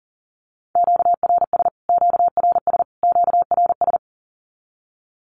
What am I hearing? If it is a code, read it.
Morse code, heard as a keyed tone, about 40 wpm, 700 Hz. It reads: QRS QRS QRS
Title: Q-Codes Repeated 3x 40wpm